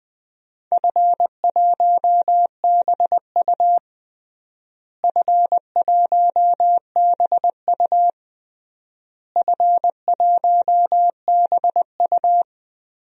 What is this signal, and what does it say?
Morse code, heard as a keyed tone, about 20 wpm, 700 Hz. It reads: F1BU F1BU F1BU
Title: DXCC-100 Call Signs - 4 Characters - Repeated 3x - 20wpm